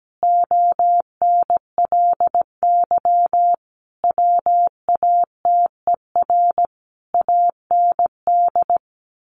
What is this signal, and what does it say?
Morse code, heard as a keyed tone, about 17 wpm, 700 Hz. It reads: ONLY WATER AND
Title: Sentences from Top 1200 Words 17wpm